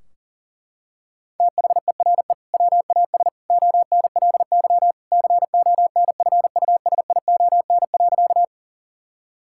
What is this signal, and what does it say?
Morse code, heard as a keyed tone, about 40 wpm, 700 Hz. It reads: THERE WAS ONLY CONFUSION.
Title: Sentences from Top 1600 Words 40wpm